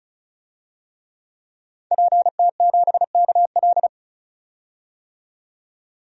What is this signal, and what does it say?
Morse code, heard as a keyed tone, about 34 wpm, 700 Hz. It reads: PT7KL
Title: DXCC-100 Call Signs - 5 Characters - 35wpm